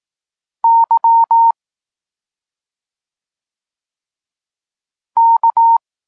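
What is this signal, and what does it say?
Morse code, heard as a keyed tone, about 18 wpm, 920 Hz. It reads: Y K
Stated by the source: Polish Morse lesson teaching the letter C, from SP9MOA Niepołomicki Klub Krótkofalowców.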